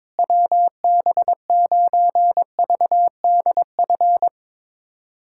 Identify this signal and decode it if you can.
Morse code, heard as a keyed tone, about 21 wpm, 700 Hz. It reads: WB9VDF
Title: USA Callsign DE Callsign - 6 Characters - 22wpm